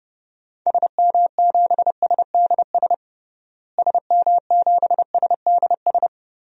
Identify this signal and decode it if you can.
Morse code, heard as a keyed tone, about 30 wpm, 700 Hz. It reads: SM7SDS SM7SDS
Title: DXCC-100 Call Signs - 6 Characters - Repeated 3x - 30wpm